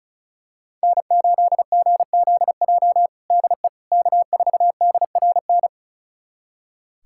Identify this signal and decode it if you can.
Morse code, heard as a keyed tone, about 36 wpm, 700 Hz. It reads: N8GZJ DE K4DRN
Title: USA Callsign DE Callsign 35wpm